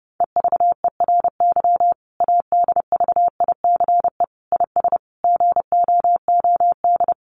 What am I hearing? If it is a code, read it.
Morse code, heard as a keyed tone, about 30 wpm, 700 Hz. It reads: EVERY ADVICE IS GOOD
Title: Sentences from Top 1100 Words 30wpm